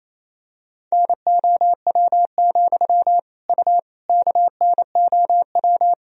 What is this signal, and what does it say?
Morse code, heard as a keyed tone, about 27 wpm, 700 Hz. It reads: NOW, U KNOW
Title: Sentences from Top 200 Words and Abbreviations 28wpm